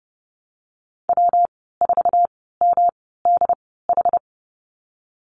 Morse code, heard as a keyed tone, about 30 wpm, spaced as for 17 wpm, 700 Hz. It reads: W4MDH
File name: Lesson 19g - Letter w - Call Signs 30wpm